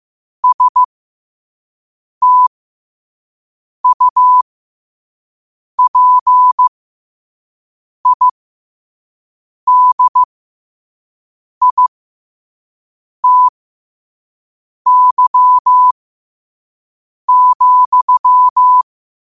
Morse code, heard as a keyed tone, about 15 wpm, spaced as for 5 wpm, 1000 Hz. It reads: STUPIDITY,
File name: qotdfiles / 5wpm